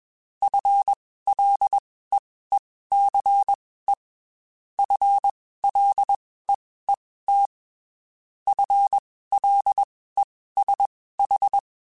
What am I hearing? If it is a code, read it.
Morse code, heard as a keyed tone, about 21 wpm, spaced as for 15 wpm, 800 Hz. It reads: FLEECE FLEET FLESH